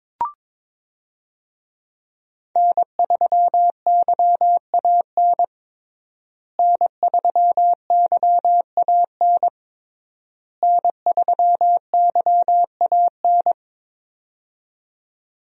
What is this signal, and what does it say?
Morse code, heard as a keyed tone, about 22 wpm, 700 Hz. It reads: N3YAN N3YAN N3YAN
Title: USA Call Signs - 5 Characters - Repeated 3x - 22wpm